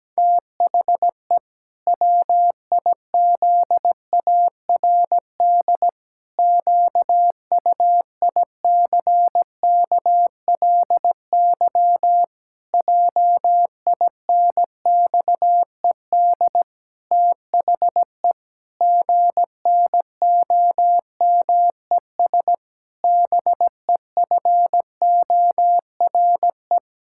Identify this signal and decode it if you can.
Morse code, heard as a keyed tone, about 17 wpm, 700 Hz. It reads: THE WIZARD QUICKLY JINXED THE GNOMES BEFORE
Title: Pangram Challenge v2 - 17wpm